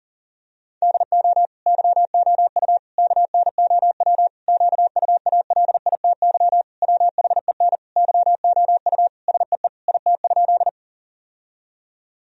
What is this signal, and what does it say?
Morse code, heard as a keyed tone, about 40 wpm, 700 Hz. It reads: DO YOU KNOW QUALITY WHEN YOU SEE IT?
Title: Sentences from Top 1000 Words - Review Part 4 - 40wpm